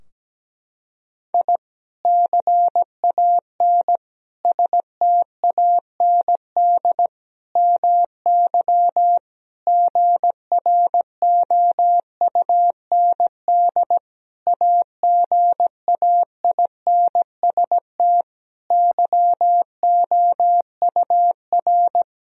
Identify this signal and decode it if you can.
Morse code, heard as a keyed tone, about 17 wpm, 700 Hz. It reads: I CAN STAND MY GROUND AGAINST YOUR
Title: Sentences from Top 1000 Words - Review Part 2 - 17wpm